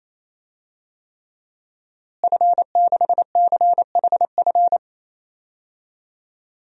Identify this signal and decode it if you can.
Morse code, heard as a keyed tone, about 28 wpm, 700 Hz. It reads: F6CHF
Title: DXCC-100 Call Signs - 5 Characters - Repeated 3x - 28wpm